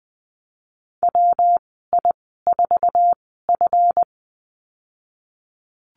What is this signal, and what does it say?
Morse code, heard as a keyed tone, about 20 wpm, spaced as for 15 wpm, 700 Hz. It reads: WI4F
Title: 154 Lesson 25g - Letter p - Call Signs 20wpm